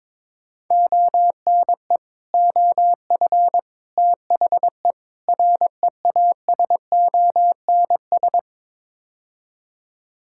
Morse code, heard as a keyed tone, about 22 wpm, 700 Hz. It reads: ONE OF THE REASONS
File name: practice / 22wpm / Build A Phrase - 5-Word Phrases - 22wpm